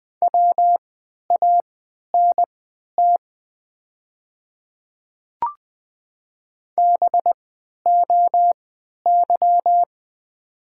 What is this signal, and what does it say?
Morse code, heard as a keyed tone, about 20 wpm, spaced as for 11 wpm, 690 Hz. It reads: WANT BOY
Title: 3x Character Spacing - Top 300 Words - 20wpm